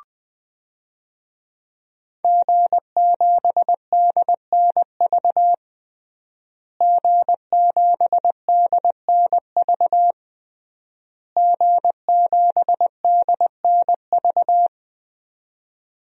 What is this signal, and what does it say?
Morse code, heard as a keyed tone, about 21 wpm, 700 Hz. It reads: G7DNV G7DNV G7DNV
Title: DXCC-100 Call Signs - 5 Characters - Repeated 3x - 20wpm